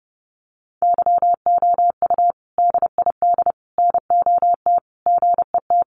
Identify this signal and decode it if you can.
Morse code, heard as a keyed tone, about 30 wpm, 700 Hz. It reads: YOU DID NOT GET
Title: Sentences from Top 600 Words 30wpm